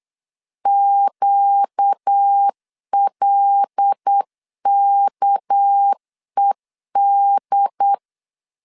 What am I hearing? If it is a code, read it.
Morse code, heard as a keyed tone, about 8 wpm, 800 Hz. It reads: QLKED
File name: LEZIONI / 9 2867 8wpm 3440